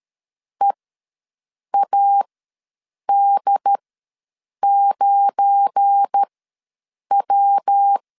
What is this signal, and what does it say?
Morse code, heard as a keyed tone, about 13 wpm, spaced as for 7 wpm, 800 Hz. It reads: EAD9W
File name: LEZIONI / 7 GF9Z / 7 GF9 ws 12wpm 3344